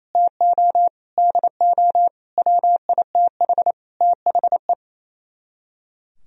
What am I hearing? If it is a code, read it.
Morse code, heard as a keyed tone, about 36 wpm, 700 Hz. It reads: TO DO WITH THE